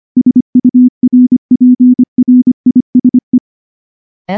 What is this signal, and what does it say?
Morse code, heard as a keyed tone, about 24 wpm, 250 Hz. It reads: SURPRISE
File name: morse-training-25WPM